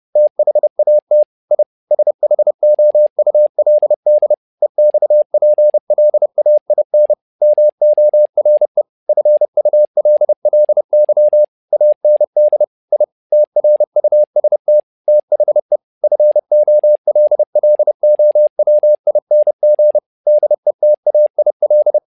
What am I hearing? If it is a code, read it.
Morse code, heard as a keyed tone, about 30 wpm, 600 Hz. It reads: THAT I SHOULD EXPLAIN MORE FULLY AND I TRUST THE FOLLOWING DETAIL
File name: learning center / tower trouble / tower trouble 30wpm